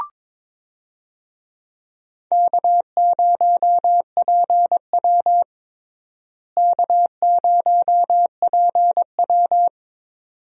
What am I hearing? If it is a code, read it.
Morse code, heard as a keyed tone, about 22 wpm, 700 Hz. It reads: K0PW K0PW
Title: USA Call Signs - 4 Characters - Repeated 3x - 22wpm